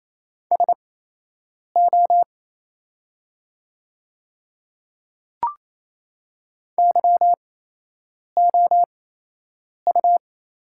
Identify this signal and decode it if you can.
Morse code, heard as a keyed tone, about 28 wpm, spaced as for 8 wpm, 700 Hz. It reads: SO YOU
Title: 8x Character Spacing - Top 100 Words - 28wpm